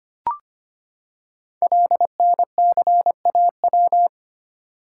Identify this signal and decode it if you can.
Morse code, heard as a keyed tone, about 25 wpm, 700 Hz. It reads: LNCAW
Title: Code Groups - 5 Letters - 25wpm